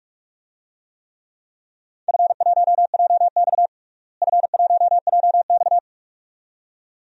Morse code, heard as a keyed tone, about 44 wpm, 700 Hz. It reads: F1JX F1JX
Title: DXCC-100 Call Signs - 4 Characters - Repeated 2x - 45wpm